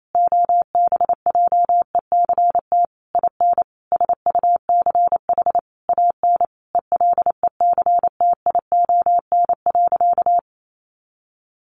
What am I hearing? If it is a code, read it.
Morse code, heard as a keyed tone, about 28 wpm, 700 Hz. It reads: OBJECT IN SUCH AN ELECTION.
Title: Sentences from Top 1500 Words 28wpm